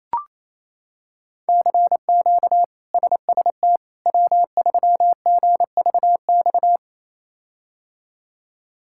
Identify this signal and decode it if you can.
Morse code, heard as a keyed tone, about 27 wpm, 700 Hz. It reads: CQ SST W3GVX
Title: K1USN SST QSO Practice - 28wpm